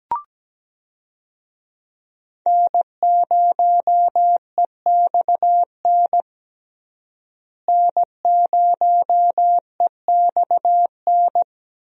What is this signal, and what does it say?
Morse code, heard as a keyed tone, about 17 wpm, 700 Hz. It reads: N0EXN N0EXN
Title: USA Call Signs - 5 Characters - Repeated 2x - 17wpm